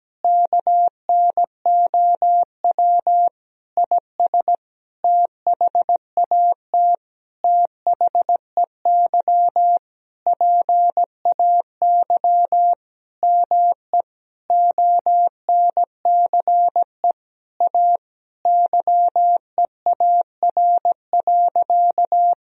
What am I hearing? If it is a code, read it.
Morse code, heard as a keyed tone, about 17 wpm, 700 Hz. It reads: KNOW IS THAT THEY PAY ME ONCE A YEAR.
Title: Sentences from Top 1000 Words - Review Part 1 - 17wpm